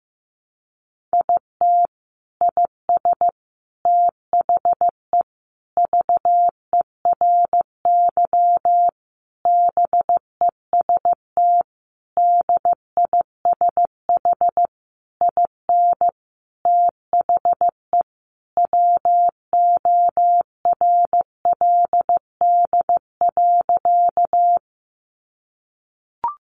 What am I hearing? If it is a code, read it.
Morse code, heard as a keyed tone, about 15 wpm, 700 Hz. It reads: IT IS THE VERY BEST DISH IN THE WORLD.
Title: Sentences from Top 1300 Words 15wpm